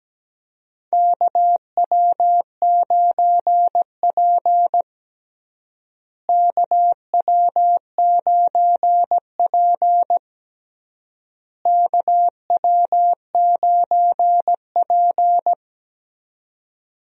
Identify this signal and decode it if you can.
Morse code, heard as a keyed tone, about 17 wpm, 700 Hz. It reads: KW9P KW9P KW9P
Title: USA Call Signs - 4 Characters - Repeated 3x - 17wpm